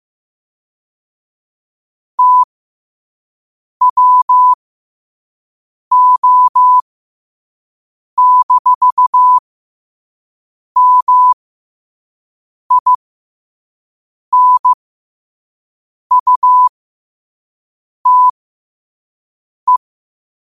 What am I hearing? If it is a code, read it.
Morse code, heard as a keyed tone, about 15 wpm, spaced as for 5 wpm, 1000 Hz. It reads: TWO-MINUTE
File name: qotdfiles / 5wpm